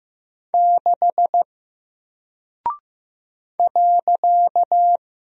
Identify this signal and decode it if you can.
Morse code, heard as a keyed tone, about 15 wpm, 700 Hz. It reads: 6 .
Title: Single Character (Letter:Number:Punctuation:Prosign) Alt 15wpm